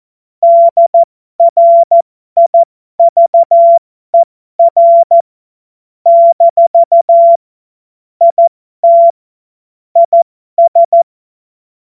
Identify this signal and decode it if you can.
Morse code, heard as a keyed tone, about 14 wpm, spaced as for 12 wpm, 680 Hz. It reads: DRIVER - IT IS
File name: Morsekurs / Witze